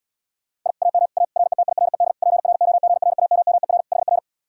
Morse code, heard as a keyed tone, about 77 wpm, 700 Hz. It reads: I AM A RETIRED PROGRAMMER HR